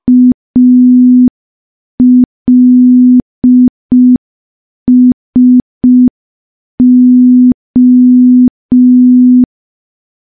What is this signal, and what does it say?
Morse code, heard as a keyed tone, about 5 wpm, 250 Hz. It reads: ALSO
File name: morse / morse-training-05WPM